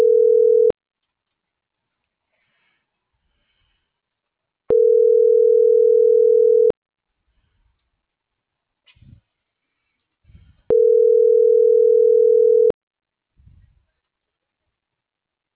Ringback tone (North America), 440 + 480 Hz.
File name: CENTRAL GENERAL / out rubyalejandra - .27134